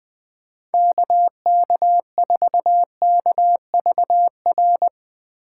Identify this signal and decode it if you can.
Morse code, heard as a keyed tone, about 20 wpm, 700 Hz. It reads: KK4KVR
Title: Speed Racing - USA Call Signs 17wpm